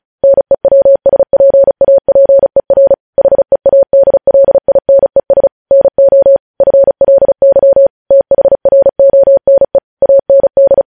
Morse code, heard as a keyed tone, about 36 wpm, 550 Hz. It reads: NEWSPAPER HEADLINES NO FLY THRONE AND
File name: podcast / News headlines-1x-35x35